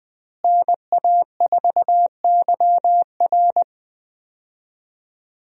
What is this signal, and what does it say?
Morse code, heard as a keyed tone, about 20 wpm, 700 Hz. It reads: NA4YR
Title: USA Call Signs - 5 Characters - Repeated 3x - 20wpm